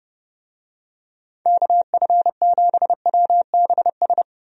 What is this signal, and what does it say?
Morse code, heard as a keyed tone, about 30 wpm, 700 Hz. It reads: KF7WBS
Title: USA Call Signs - 6 Characters - Repeated 3x - 30wpm